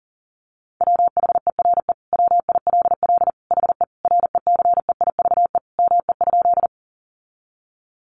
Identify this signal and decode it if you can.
Morse code, heard as a keyed tone, about 40 wpm, 700 Hz. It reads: WHERE WILL HE RECEIVE ME?
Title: Sentences from Top 1000 Words 40wpm